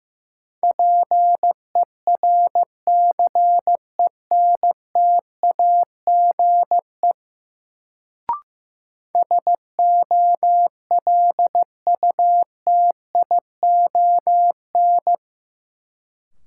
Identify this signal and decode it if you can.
Morse code, heard as a keyed tone, about 15 wpm, 700 Hz. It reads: PERCENTAGE SOLUTION
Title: Sentences from Top 1300 Words - New Word Drill Alt - 15wpm